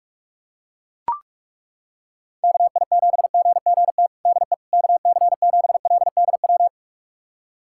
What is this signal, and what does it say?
Morse code, heard as a keyed tone, about 41 wpm, 700 Hz. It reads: KI7GGT DE KC7LDW
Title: USA Callsign DE Callsign - 6 Characters - 45wpm